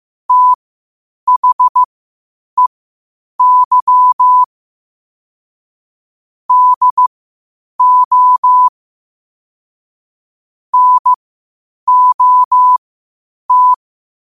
Morse code, heard as a keyed tone, about 15 wpm, spaced as for 8 wpm, 1000 Hz. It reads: THEY DO NOT
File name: qotdfiles / 8wpm